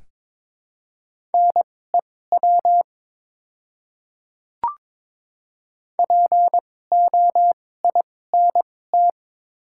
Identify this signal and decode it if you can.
Morse code, heard as a keyed tone, about 22 wpm, spaced as for 16 wpm, 700 Hz. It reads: NEW POINT
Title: 2x Character Spacing - Top 200 Words - 22wpm